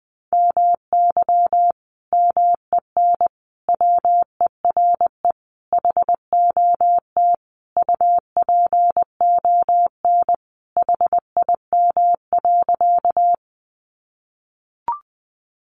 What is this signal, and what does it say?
Morse code, heard as a keyed tone, about 20 wpm, 700 Hz. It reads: MY MEN WERE HOT UPON HIM.
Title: Sentences from Top 1100 Words 20wpm